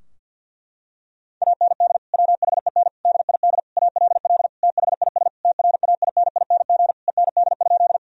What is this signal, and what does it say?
Morse code, heard as a keyed tone, about 50 wpm, 700 Hz. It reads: AND WHEN DID ALL THIS TRAINING END?
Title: Sentences from Top 1000 Words - Review Part 4 - 50wpm